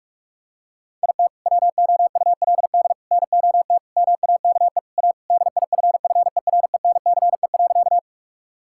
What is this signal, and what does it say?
Morse code, heard as a keyed tone, about 45 wpm, 700 Hz. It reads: IT WOULD NOT MAKE A DIFFERENCE.
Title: Sentences from Top 1000 Words - Review Part 5 - 45wpm